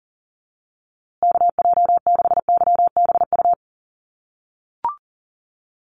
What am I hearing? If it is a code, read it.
Morse code, heard as a keyed tone, about 40 wpm, 700 Hz. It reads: KJ6YBU
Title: USA Call Signs Repeated 2x 40wpm